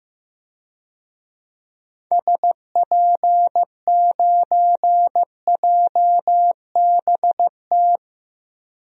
Morse code, heard as a keyed tone, about 16 wpm, 700 Hz. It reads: SP9JBT
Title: DXCC-100 Call Signs - 6 Characters - Repeated 3x - 15wpm